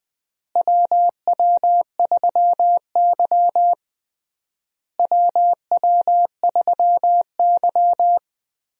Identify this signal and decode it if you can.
Morse code, heard as a keyed tone, about 21 wpm, 700 Hz. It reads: WW3Y WW3Y
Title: USA Call Signs - 4 Characters - Repeated 3x - 20wpm